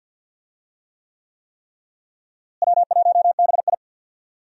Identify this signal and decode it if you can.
Morse code, heard as a keyed tone, about 52 wpm, 700 Hz. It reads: W1BI
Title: USA Call Signs - 4 Characters - Repeated 3x - 50wpm